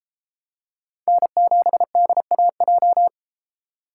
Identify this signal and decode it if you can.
Morse code, heard as a keyed tone, about 32 wpm, 700 Hz. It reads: N7DAJ